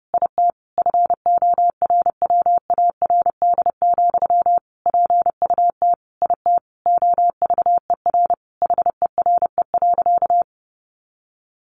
Morse code, heard as a keyed tone, about 30 wpm, 700 Hz. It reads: IT FORWARD, PUT IT OVER HERE.
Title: Sentences from Top 1500 Words 30wpm